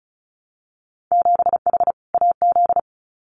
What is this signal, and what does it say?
Morse code, heard as a keyed tone, about 34 wpm, 700 Hz. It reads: 7H AZ
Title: Winter Field Day Exchanges Encore 35wpm